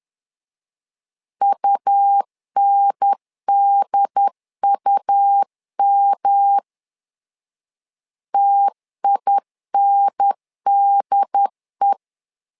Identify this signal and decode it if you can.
Morse code, heard as a keyed tone, about 11 wpm, 800 Hz. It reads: UNDUM TINDE
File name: LEZIONI / 3 MWRO / 3 M 10wpm 1828